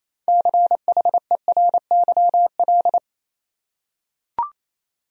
Morse code, heard as a keyed tone, about 28 wpm, 700 Hz. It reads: CHERYL
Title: USA Common Names - 28wpm